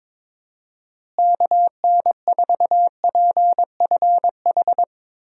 Morse code, heard as a keyed tone, about 21 wpm, 700 Hz. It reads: KN4PFH
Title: USA Call Signs - 6 Characters - Repeated 3x - 22wpm